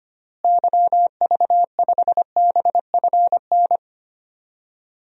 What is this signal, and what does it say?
Morse code, heard as a keyed tone, about 25 wpm, 700 Hz. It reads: YV5BFN
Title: DXCC-100 Call Signs - 6 Characters - Repeated 2x - 25wpm